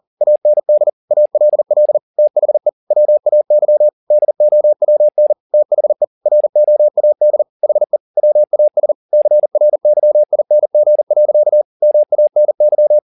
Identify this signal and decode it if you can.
Morse code, heard as a keyed tone, about 40 wpm, 600 Hz. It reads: AND ALL THE WAY DOWN THE ROAD HE WAS CRYING. MANY
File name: the old man and the sea 2 40wpm